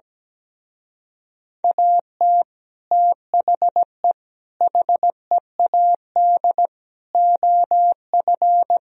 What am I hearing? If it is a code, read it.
Morse code, heard as a keyed tone, about 17 wpm, 700 Hz. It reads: AT THE HEAD OF
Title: Speed-Racing - Sets of 5 Words 17wpm